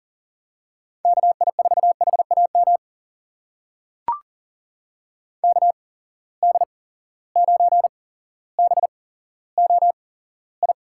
Morse code, heard as a keyed tone, about 40 wpm, spaced as for 17 wpm, 700 Hz. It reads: KI4HAM KD9BOI
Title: USA Call Signs - 6 Characters - 8x Character Spacing - 40wpm